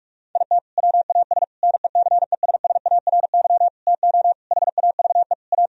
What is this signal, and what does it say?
Morse code, heard as a keyed tone, about 46 wpm, 700 Hz. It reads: IT WAS NECESSARY TO HAVE A
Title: Sentences from Top 1000 Words - Review Part 1 - 45wpm